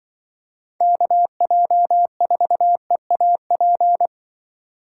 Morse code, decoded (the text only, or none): KJ4EAP